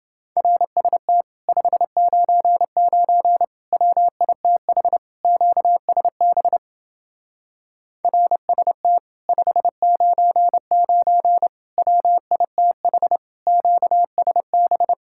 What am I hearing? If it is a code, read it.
Morse code, heard as a keyed tone, about 31 wpm, 700 Hz. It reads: RST 599 WITH QSB RST 599 WITH QSB